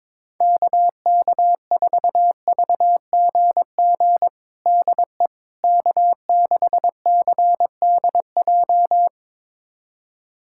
Morse code, heard as a keyed tone, about 22 wpm, 700 Hz. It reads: KK4VGG DE K6CDJ
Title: USA Callsign DE Callsign 22wpm